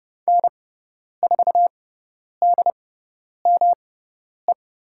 Morse code, heard as a keyed tone, about 30 wpm, spaced as for 10 wpm, 700 Hz. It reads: N4DME
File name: Lesson 18g - Letter m - Call Signs 30wpm